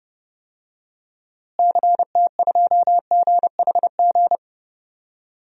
Morse code, heard as a keyed tone, about 31 wpm, 700 Hz. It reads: CT2GHG
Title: DXCC-100 Call Signs - 6 Characters - Repeated 3x - 30wpm